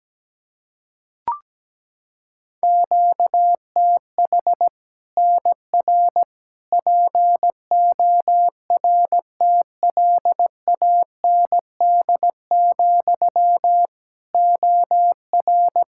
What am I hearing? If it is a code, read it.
Morse code, heard as a keyed tone, about 17 wpm, 700 Hz. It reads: QTH NR PORTLAND, OR
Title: Practice QSO - Exchange 1 - Beginning - 17wpm